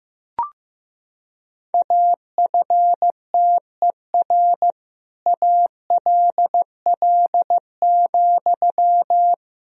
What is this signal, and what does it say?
Morse code, heard as a keyed tone, about 15 wpm, 700 Hz. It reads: AFTER ALL,
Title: Sentences from Top 1000 Words - Review Part 4 - 15wpm